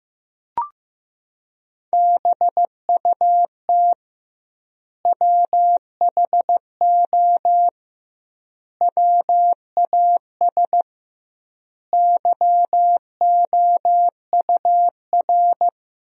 Morse code, heard as a keyed tone, about 15 wpm, 700 Hz. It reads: BUT WHO WAS YOUR
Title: 2x Word-Spacing - Sentences from Top 200 Words - 15wpm